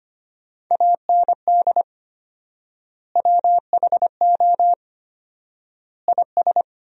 Morse code, heard as a keyed tone, about 25 wpm, 700 Hz. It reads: AND WHO IS
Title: 4x Word-Spacing - Sentences from Top 300 Words - 25wpm